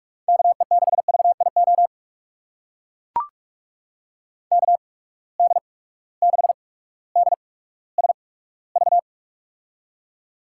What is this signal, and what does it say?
Morse code, heard as a keyed tone, about 42 wpm, spaced as for 19 wpm, 700 Hz. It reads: KE6VIO KD6DSV
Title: USA Call Signs - 6 Characters - 8x Character Spacing - 45wpm